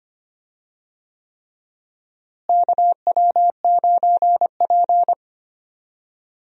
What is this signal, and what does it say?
Morse code, heard as a keyed tone, about 25 wpm, 700 Hz. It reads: KW9P
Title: USA Call Signs - 4 Characters - Repeated 3x - 25wpm